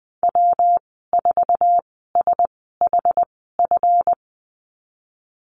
Morse code, heard as a keyed tone, about 20 wpm, spaced as for 15 wpm, 700 Hz. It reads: W4SHF
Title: 161 Lesson 26g - Letter g - Call Signs 20wpm